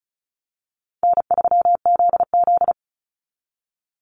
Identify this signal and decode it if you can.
Morse code, heard as a keyed tone, about 34 wpm, 700 Hz. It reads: N3ZZ
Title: Call Signs Repeated 2x - 35wpm